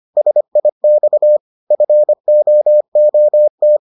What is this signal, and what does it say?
Morse code, heard as a keyed tone, about 25 wpm, 600 Hz. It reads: SIX FOOT